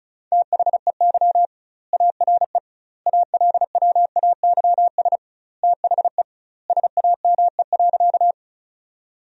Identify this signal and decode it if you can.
Morse code, heard as a keyed tone, about 36 wpm, 700 Hz. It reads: THEY ARE ALWAYS THE SAME.
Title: 2x Word-Spacing - Sentences from Top 300 Words - 35wpm